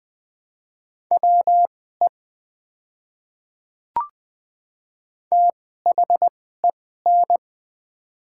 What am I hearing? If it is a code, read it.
Morse code, heard as a keyed tone, about 20 wpm, spaced as for 15 wpm, 690 Hz. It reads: WE THEN
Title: 2x Character Spacing - Top 100 Words - 20wpm